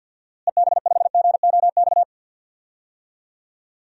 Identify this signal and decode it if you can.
Morse code, heard as a keyed tone, about 51 wpm, 700 Hz. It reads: EB5GOX
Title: DXCC-100 Call Signs - 6 Characters - Repeated 3x - 50wpm